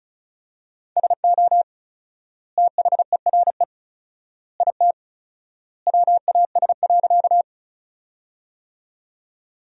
Morse code, heard as a keyed tone, about 36 wpm, 700 Hz. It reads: SO THERE IT WAS.
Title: 4x Word-Spacing - Sentences from Top 100 Words - 35wpm